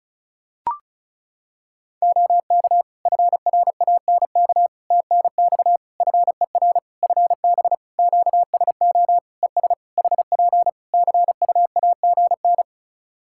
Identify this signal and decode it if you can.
Morse code, heard as a keyed tone, about 34 wpm, 700 Hz. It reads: OK FRANK TNX FER FB QSO ES HP CUAGN
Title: Practice QSO - Exchange 4 - Ending - 35wpm